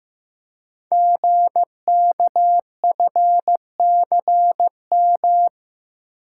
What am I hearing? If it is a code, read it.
Morse code, heard as a keyed tone, about 15 wpm, 700 Hz. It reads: GKFCM
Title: Code Groups - 5 Letters - 15wpm